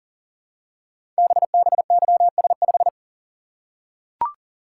Morse code, heard as a keyed tone, about 42 wpm, 700 Hz. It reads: BBYS5
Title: Code Groups - 5 Letter:Numbers - 40wpm